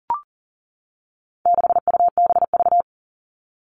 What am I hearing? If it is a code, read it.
Morse code, heard as a keyed tone, about 40 wpm, 700 Hz. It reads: 6UBV
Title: All Cut Numbers - 4-Digit 40wpm